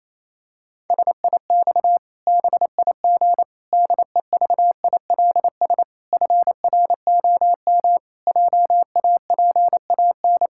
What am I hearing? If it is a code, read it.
Morse code, heard as a keyed tone, about 28 wpm, 700 Hz. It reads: SIX BIG DEVILS FROM JAPAN
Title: Pangram Challenge v2 - 28wpm